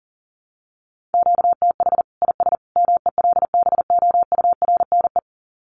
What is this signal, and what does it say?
Morse code, heard as a keyed tone, about 37 wpm, 700 Hz. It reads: QTH IS MELBOURNE
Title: Common QSO Phrases 40wpm